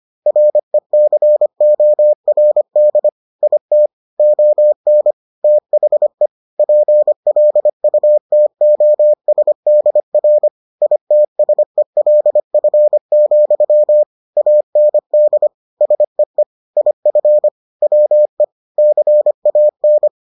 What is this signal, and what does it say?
Morse code, heard as a keyed tone, about 25 wpm, 600 Hz. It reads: RECORD IT ON THE PLUTOSDR ITSELF, AND SEE IF WE CAN